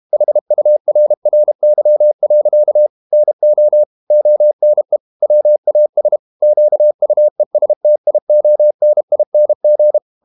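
Morse code, heard as a keyed tone, about 32 wpm, 600 Hz. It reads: HURRY. NO ONE WAS QUESTIONING
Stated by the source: Morse code practice at 32 wpm from CW med Gnister, reading Ken Follett's novel Never in English.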